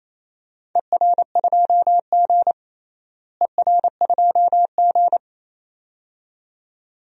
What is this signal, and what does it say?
Morse code, heard as a keyed tone, about 28 wpm, 700 Hz. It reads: ER2G ER2G
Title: DXCC-100 Call Signs - 4 Characters - Repeated 2x - 28wpm